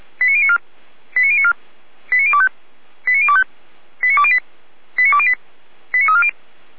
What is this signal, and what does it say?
Five-tone selective call (ZVEI), heard as quick runs of five tones, tones of about 70 ms. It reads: 89094 89094 89025 89025 89208 89208 89340